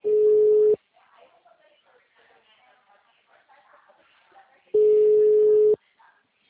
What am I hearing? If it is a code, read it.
Ringback tone (Europe, 425 Hz), with speech.